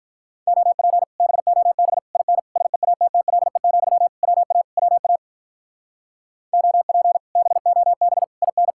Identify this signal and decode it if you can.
Morse code, heard as a keyed tone, about 46 wpm, 700 Hz. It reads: OP BOB IN SEATTLE, WA WA OP BOB IN